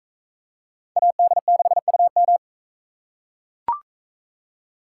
Morse code, heard as a keyed tone, about 42 wpm, 700 Hz. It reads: AD6UM